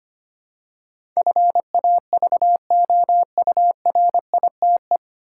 Morse code, heard as a keyed tone, about 25 wpm, 700 Hz. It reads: FAVOURITE
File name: practice / 25wpm / 9 Letter Words - First 3 Letters Given - 25wpm